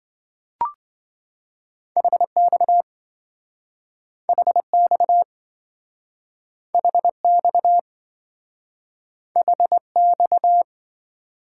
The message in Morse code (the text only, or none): HX HX HX HX